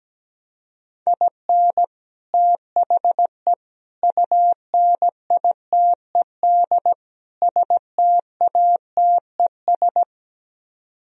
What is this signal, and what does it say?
Morse code, heard as a keyed tone, about 17 wpm, 700 Hz. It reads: IN THE UNITED STATES
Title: Speed-Racing - Sets of 4 Words 17wpm